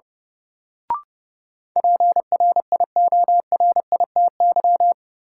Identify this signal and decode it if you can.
Morse code, heard as a keyed tone, about 30 wpm, 700 Hz. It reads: PRIORITY
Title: Sentences from Top 1600 Words - New Word Drill Alt - 30wpm